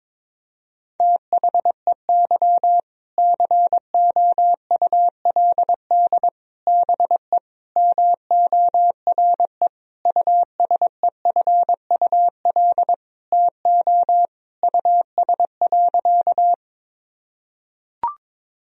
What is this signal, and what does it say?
Morse code, heard as a keyed tone, about 23 wpm, 700 Hz. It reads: THEY COULD BE MORE USEFUL TO US.
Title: Sentences from Top 1000 Words - Review Part 4 - 22wpm